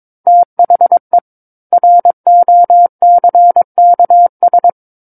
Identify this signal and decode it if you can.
Morse code, heard as a keyed tone, about 22 wpm, 700 Hz. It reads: THE ROCKS
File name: Book One / 13-Chapter 13 - 22 WPM